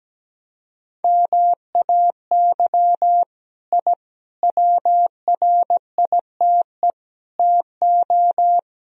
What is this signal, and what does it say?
Morse code, heard as a keyed tone, about 17 wpm, 700 Hz. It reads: MAY I WRITE TO
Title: Sentences from Top 1000 Words - Review Part 2 - 17wpm